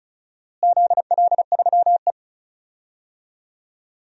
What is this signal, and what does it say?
Morse code, heard as a keyed tone, about 34 wpm, 700 Hz. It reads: ZL3E
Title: DXCC-100 Call Signs - 4 Characters - Repeated 3x - 35wpm